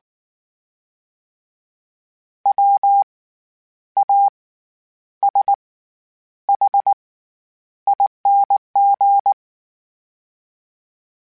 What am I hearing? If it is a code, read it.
Morse code, heard as a keyed tone, about 19 wpm, spaced as for 9 wpm, 800 Hz. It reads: WASHING
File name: Special INGs / Special INGs-mp3s / SlowToFaster ing @18